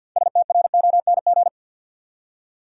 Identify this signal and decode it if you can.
Morse code, heard as a keyed tone, about 53 wpm, 700 Hz. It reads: STRONG
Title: Six Letter Words - 50wpm